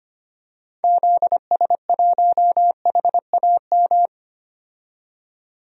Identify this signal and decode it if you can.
Morse code, heard as a keyed tone, about 25 wpm, 700 Hz. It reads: ZS1HAM
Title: DXCC-100 Call Signs - 6 Characters - 25wpm